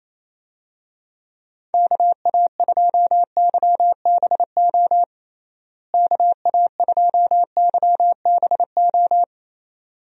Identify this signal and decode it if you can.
Morse code, heard as a keyed tone, about 27 wpm, 700 Hz. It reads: KA2YBO KA2YBO
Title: USA Call Signs - 6 Characters - Repeated 2x - 28wpm